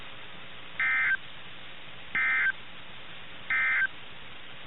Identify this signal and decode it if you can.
Emergency Alert System (SAME) tones, heard as harsh buzzing bursts. It is heard from 0.7 to 4.0 s.
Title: NOAA Weather Radio - KIH61